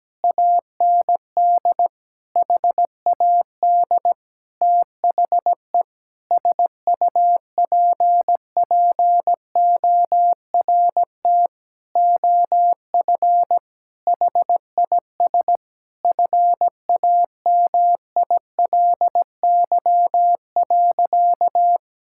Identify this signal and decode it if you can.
Morse code, heard as a keyed tone, about 17 wpm, 700 Hz. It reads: AND HAD THE SUPPORT OF HIS FAMILY.
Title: Sentences from Top 1000 Words - Review Part 5 - 17wpm